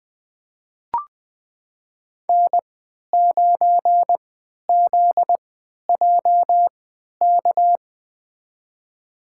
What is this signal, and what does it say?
Morse code, heard as a keyed tone, about 20 wpm, spaced as for 11 wpm, 700 Hz. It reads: N9ZJK
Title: USA Call Signs - 5 Characters - 3x Character Spacing - 20wpm